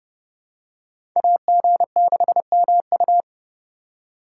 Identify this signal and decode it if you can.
Morse code, heard as a keyed tone, about 30 wpm, 700 Hz. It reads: AG6MU
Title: USA Call Signs - 5 Characters - 3x Character Spacing - 30wpm